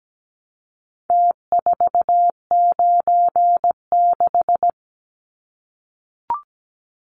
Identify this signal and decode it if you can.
Morse code, heard as a keyed tone, about 17 wpm, 700 Hz. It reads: T496
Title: Numbers - Contest Style - 4-Digit 17wpm